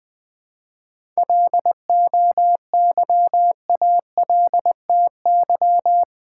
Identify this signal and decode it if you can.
Morse code, heard as a keyed tone, about 20 wpm, 700 Hz. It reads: LOYALTY
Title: 7 Letter Words - First 3 Letters Given - 20wpm